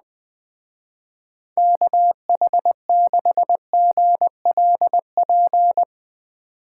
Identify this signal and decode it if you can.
Morse code, heard as a keyed tone, about 20 wpm, 700 Hz. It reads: KH6GLP